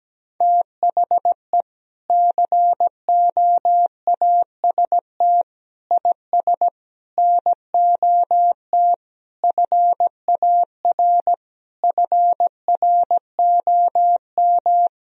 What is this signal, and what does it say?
Morse code, heard as a keyed tone, about 17 wpm, 700 Hz. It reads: THE COAST IS NOT FAR FROM
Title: Sentences from Top 1000 Words - Review Part 4 - 17wpm